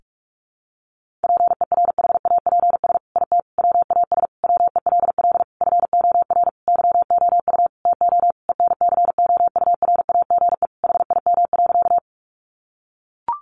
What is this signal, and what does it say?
Morse code, heard as a keyed tone, about 44 wpm, 700 Hz. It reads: PERHAPS IT WAS WELL FOR YOU TO ENCOURAGE HIM.
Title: Sentences from Top 1400 Words 45wpm